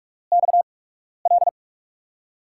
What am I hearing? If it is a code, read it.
Morse code, heard as a keyed tone, about 44 wpm, spaced as for 12 wpm, 700 Hz. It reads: XL
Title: USA Call Signs - 6 Characters - 8x Character Spacing - 45wpm